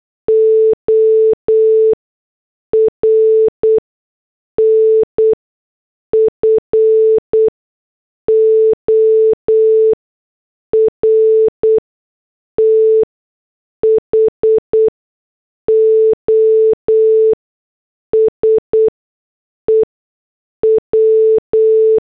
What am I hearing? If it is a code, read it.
Morse code, heard as a keyed tone, about 8 wpm, spaced as for 6 wpm, 440 Hz. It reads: ORN FOR THOSE W